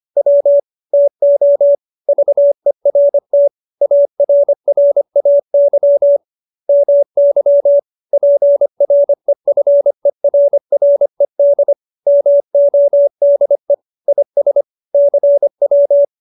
Morse code, heard as a keyed tone, about 25 wpm, 570 Hz. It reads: W TO VERT ARRAY MY PREFERRED MODE IS CW
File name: QSO315 25